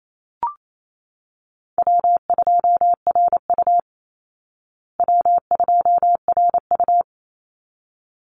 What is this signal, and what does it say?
Morse code, heard as a keyed tone, about 28 wpm, 700 Hz. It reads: W2RU W2RU
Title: Call Signs Repeated 3x - 28wpm